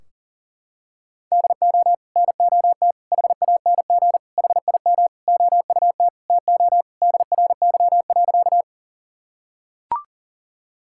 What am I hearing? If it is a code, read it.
Morse code, heard as a keyed tone, about 40 wpm, 700 Hz. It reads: DO NOT HANG HIM OUT TO DRY.
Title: Sentences from Top 1600 Words 40wpm